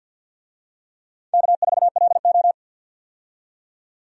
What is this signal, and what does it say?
Morse code, heard as a keyed tone, about 49 wpm, 700 Hz. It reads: K4LO
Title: USA Call Signs - 4 Characters - 50wpm